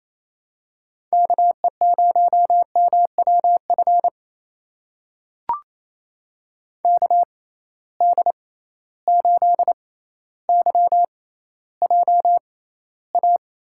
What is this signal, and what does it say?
Morse code, heard as a keyed tone, about 28 wpm, spaced as for 14 wpm, 700 Hz. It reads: KE0MWF KD8YJA
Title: USA Call Signs - 6 Characters - 6x Character Spacing - 28wpm